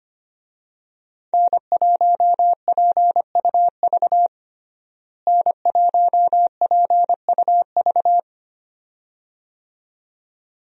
Morse code, heard as a keyed tone, about 25 wpm, 700 Hz. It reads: N1PUV N1PUV
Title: USA Call Signs - 5 Characters - Repeated 2x - 25wpm